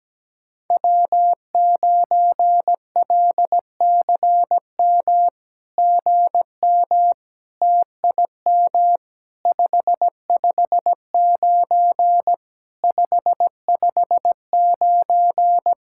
Morse code, heard as a keyed tone, about 17 wpm, 700 Hz. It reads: W9LCM GM TIM 559 559
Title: SOTA QSO Practice 17wpm